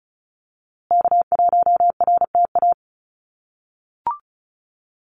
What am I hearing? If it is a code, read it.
Morse code, heard as a keyed tone, about 34 wpm, 700 Hz. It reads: K1RTA
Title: USA Call Signs Repeated 2x 35wpm